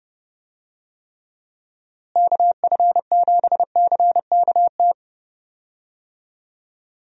Morse code, heard as a keyed tone, about 30 wpm, 700 Hz. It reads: KF7CKT
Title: USA Call Signs - 6 Characters - Repeated 2x - 30wpm